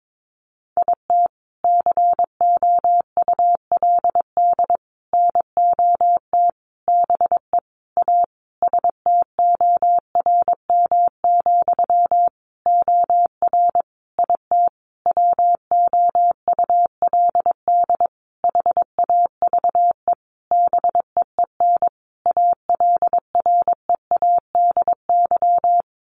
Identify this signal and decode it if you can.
Morse code, heard as a keyed tone, about 22 wpm, 700 Hz. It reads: IT COULD NOT BE A STORM, OR IT WOULD HAVE BEEN ALREADY
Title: Sentences from Top 1500 Words 22wpm